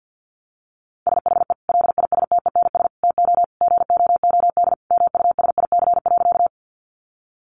Morse code, heard as a keyed tone, about 50 wpm, 700 Hz. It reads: SHE LISTENS TO GOOD MUSIC.
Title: Easy Phrases 50wpm